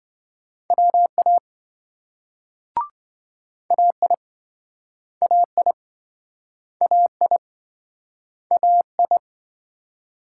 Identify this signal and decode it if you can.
Morse code, heard as a keyed tone, about 26 wpm, 700 Hz. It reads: WA AI AI AI AI